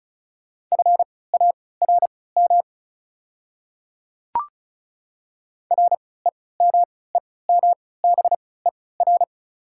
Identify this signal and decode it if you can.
Morse code, heard as a keyed tone, about 36 wpm, spaced as for 20 wpm, 690 Hz. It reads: FARM REMEMBER
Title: 3x Character Spacing - Top 500 Words - 35wpm